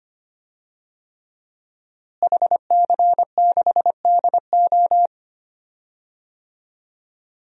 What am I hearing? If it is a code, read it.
Morse code, heard as a keyed tone, about 25 wpm, 700 Hz. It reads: HC6DO
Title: DXCC-100 Call Signs - 5 Characters - Repeated 3x - 25wpm